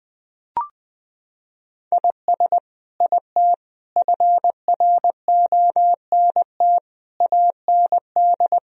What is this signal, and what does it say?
Morse code, heard as a keyed tone, about 20 wpm, 700 Hz. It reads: IS IT FRONT AND
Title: Sentences from Top 1000 Words - Review Part 2 - 20wpm